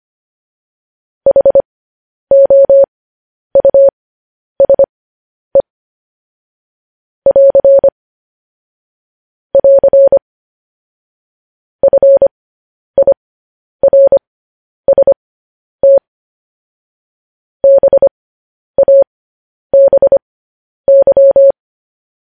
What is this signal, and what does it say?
Morse code, heard as a keyed tone, about 25 wpm, spaced as for 10 wpm, 550 Hz. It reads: HOUSE <AR> <AR> FIRST BABY
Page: news headlines in morse code at 10 wpm